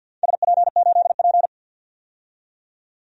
Morse code, heard as a keyed tone, about 49 wpm, 700 Hz. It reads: SP8P
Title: DXCC-100 Call Signs - 4 Characters - 50wpm